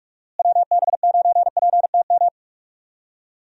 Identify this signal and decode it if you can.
Morse code, heard as a keyed tone, about 46 wpm, 700 Hz. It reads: WB9PTM
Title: USA Call Signs - 6 Characters - Repeated 3x - 45wpm